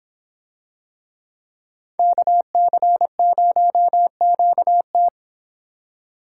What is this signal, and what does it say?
Morse code, heard as a keyed tone, about 26 wpm, 700 Hz. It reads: KC0QT